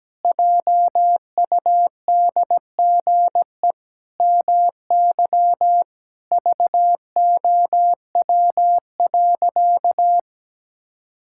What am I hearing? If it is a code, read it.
Morse code, heard as a keyed tone, about 17 wpm, 700 Hz. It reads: JUDGE MY VOW.
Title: Pangram Challenge v2 - 17wpm